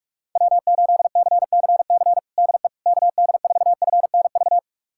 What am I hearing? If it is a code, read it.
Morse code, heard as a keyed tone, about 46 wpm, 690 Hz. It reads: W8CCC DE KD4FNV
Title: USA Callsign DE Callsign 45wpm